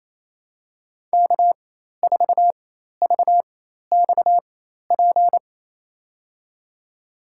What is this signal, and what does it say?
Morse code, heard as a keyed tone, about 27 wpm, spaced as for 13 wpm, 700 Hz. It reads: K4VXP
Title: USA Call Signs - 5 Characters - 4x Character Spacing - 28wpm